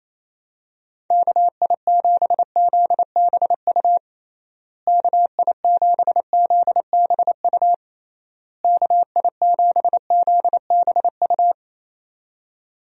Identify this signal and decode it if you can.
Morse code, heard as a keyed tone, about 28 wpm, 700 Hz. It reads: KI7ZBU KI7ZBU KI7ZBU
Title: USA Call Signs - 6 Characters - Repeated 3x - 28wpm